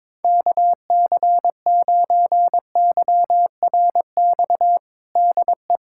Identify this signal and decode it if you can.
Morse code, heard as a keyed tone, about 22 wpm, 700 Hz. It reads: KC9YRX DE
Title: USA Callsign DE Callsign 22wpm